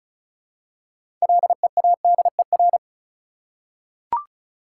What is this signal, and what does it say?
Morse code, heard as a keyed tone, about 36 wpm, 700 Hz. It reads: LEADER
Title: Six Letter Words - 35wpm